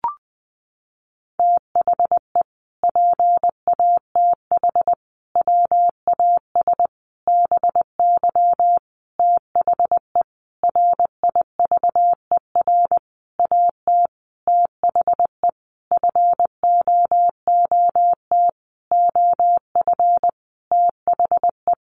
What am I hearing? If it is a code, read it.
Morse code, heard as a keyed tone, about 20 wpm, 700 Hz. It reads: THE PATH WAS BY THE RIVER AT THE FOOT OF THE
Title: Sentences from Top 1400 Words 20wpm